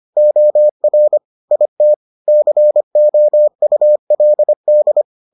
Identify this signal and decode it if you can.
Morse code, heard as a keyed tone, about 25 wpm, 600 Hz. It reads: OR IT COULD